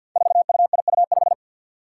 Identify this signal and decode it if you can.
Morse code, heard as a keyed tone, about 52 wpm, 700 Hz. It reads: 4UIU5
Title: Five Challenging Letter-Number Combinations 50wpm